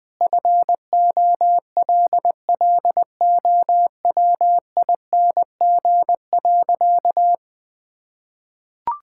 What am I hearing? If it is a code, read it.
Morse code, heard as a keyed tone, about 21 wpm, 700 Hz. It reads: FOLLOWING.
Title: Sentences with 9-Word Phrases - Spoken Beginning - 20wpm